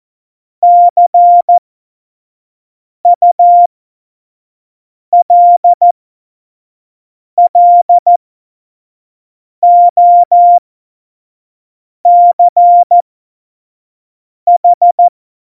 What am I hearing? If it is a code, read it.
Morse code, heard as a keyed tone, about 14 wpm, spaced as for 5 wpm, 700 Hz. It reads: CULLOCH
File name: Morsekurs / Witze / Witz 03. November 2020, 25 F70